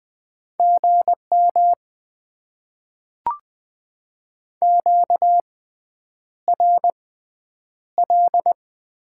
Morse code, heard as a keyed tone, about 20 wpm, spaced as for 9 wpm, 700 Hz. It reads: GM QRL
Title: Top 90 QSO Elements - 6x Character Spacing - 20wpm